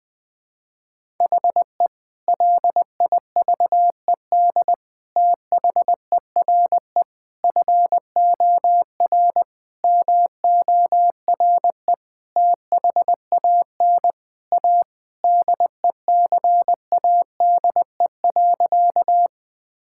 Morse code, heard as a keyed tone, about 19 wpm, 700 Hz. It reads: HE LIVED THERE FOR MORE THAN A DECADE.
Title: Sentences with 5-Word Phrases - 20wpm